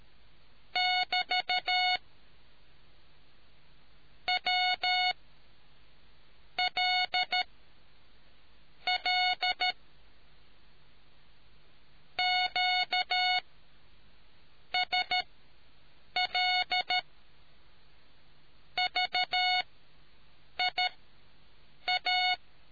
Morse code, heard as a keyed tone, about 13 wpm, spaced as for 6 wpm, 2200 Hz. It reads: <BT> W L L QSL VIA